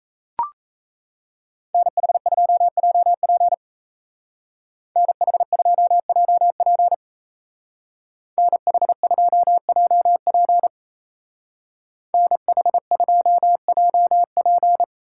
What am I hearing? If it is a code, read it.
Morse code, heard as a keyed tone, about 36 wpm, 700 Hz. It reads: NH2JP NH2JP NH2JP NH2JP